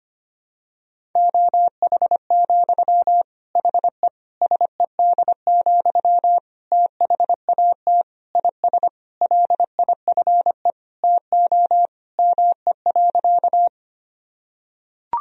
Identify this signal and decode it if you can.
Morse code, heard as a keyed tone, about 25 wpm, 700 Hz. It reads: OH, HE SED, THAT IS LIFE TO ME.
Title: Sentences from Top 500 Words and Abbreviations 25wpm